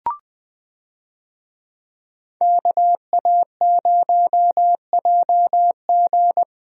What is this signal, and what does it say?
Morse code, heard as a keyed tone, about 20 wpm, 700 Hz. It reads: KA0JG
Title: USA Call Signs - 5 Characters - Repeated 3x - 20wpm